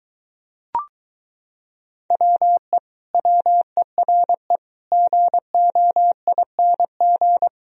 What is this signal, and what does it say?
Morse code, heard as a keyed tone, about 23 wpm, 700 Hz. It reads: WE WERE GOING